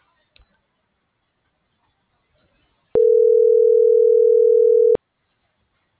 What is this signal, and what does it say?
Ringback tone (North America), 440 + 480 Hz.